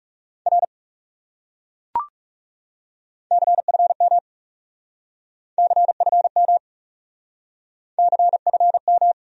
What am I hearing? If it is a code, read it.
Morse code, heard as a keyed tone, about 39 wpm, 700 Hz. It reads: R CFM CFM CFM